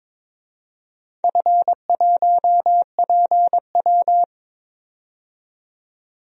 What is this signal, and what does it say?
Morse code, heard as a keyed tone, about 22 wpm, 700 Hz. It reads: F1PW
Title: DXCC-100 Call Signs - 4 Characters - 22wpm